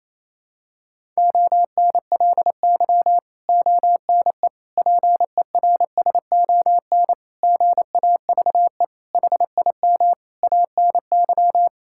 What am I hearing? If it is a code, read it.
Morse code, heard as a keyed tone, about 28 wpm, 700 Hz. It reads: ONLY ONE PERSON GAVE HIM ANY
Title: Sentences from Top 1000 Words - Review Part 6 - 28wpm